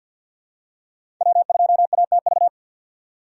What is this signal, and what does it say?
Morse code, heard as a keyed tone, about 47 wpm, 700 Hz. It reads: W2ATV
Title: USA Call Signs - 5 Characters - 3x Character Spacing - 50wpm